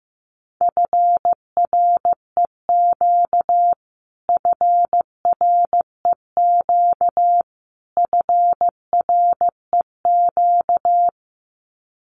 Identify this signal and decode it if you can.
Morse code, heard as a keyed tone, about 15 wpm, 700 Hz. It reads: FREQ FREQ FREQ
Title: Abbreviations Repeated 3x - 15wpm